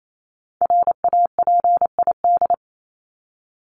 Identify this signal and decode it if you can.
Morse code, heard as a keyed tone, about 28 wpm, 700 Hz. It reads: RAPID
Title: LY Suffix - Top 100 Words with Base Words 28wpm